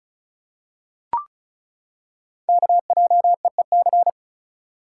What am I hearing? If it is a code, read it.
Morse code, heard as a keyed tone, about 33 wpm, 700 Hz. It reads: KJEEC
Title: Code Groups - 5 Letters - 35wpm